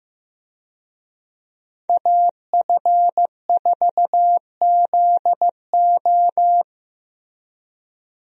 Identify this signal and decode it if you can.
Morse code, heard as a keyed tone, about 15 wpm, 700 Hz. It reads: AF4ZO
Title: USA Call Signs - 5 Characters - Repeated 3x - 15wpm